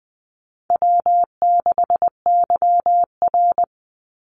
Morse code, heard as a keyed tone, about 20 wpm, 700 Hz. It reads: W6YR
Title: Easy Call Signs Repeated 2x - 20wpm